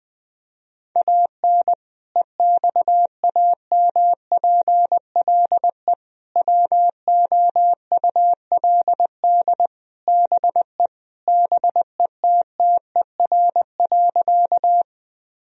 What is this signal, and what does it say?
Morse code, heard as a keyed tone, about 21 wpm, 700 Hz. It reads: AN EXAMPLE WOULD BE BETTER.
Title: Sentences from Top 1000 Words - Review Part 1 - 20wpm